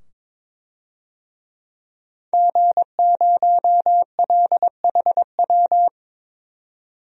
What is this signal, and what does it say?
Morse code, heard as a keyed tone, about 22 wpm, 700 Hz. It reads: G0LHW
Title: DXCC-100 Call Signs - 5 Characters - 22wpm